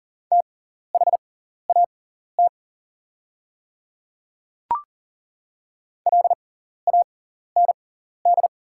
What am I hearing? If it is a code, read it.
Morse code, heard as a keyed tone, about 40 wpm, spaced as for 14 wpm, 700 Hz. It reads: THAT LAND
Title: 6x Character Spacing - Top 200 Words - 40wpm